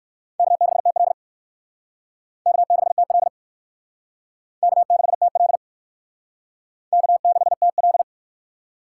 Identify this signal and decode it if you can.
Morse code, heard as a keyed tone, about 54 wpm, 700 Hz. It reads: K6TL K6TL K6TL K6TL
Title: Speed Racing - USA 1x2 Call Signs 45wpm